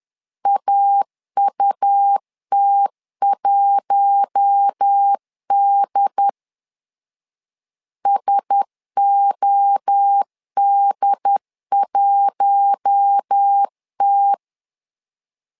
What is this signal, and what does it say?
Morse code, heard as a keyed tone, about 10 wpm, 800 Hz. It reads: AUT1D SOD1T
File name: LEZIONI / 4 Q1S 10wpm 2133